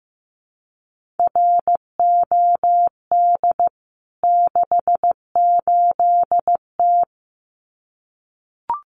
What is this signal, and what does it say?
Morse code, heard as a keyed tone, about 15 wpm, 700 Hz. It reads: ROD 68T
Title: CWT Exchange 15wpm